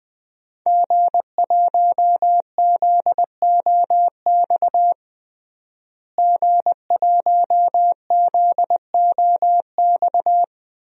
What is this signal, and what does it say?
Morse code, heard as a keyed tone, about 20 wpm, 700 Hz. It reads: G1ZOX G1ZOX
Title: DXCC-100 Call Signs - 5 Characters - Repeated 3x - 20wpm